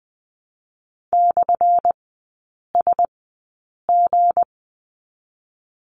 Morse code, heard as a keyed tone, about 20 wpm, spaced as for 8 wpm, 700 Hz. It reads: /SG
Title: 218 Lesson 35e - Number 0 - Three Characters 20wpm